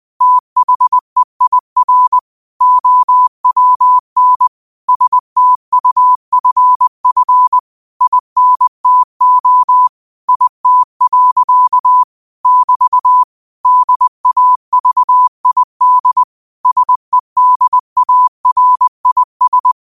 Morse code, heard as a keyed tone, about 20 wpm, 1000 Hz. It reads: THEIR OWN STUFF INTO IT. <BT> DAVID SEDARIS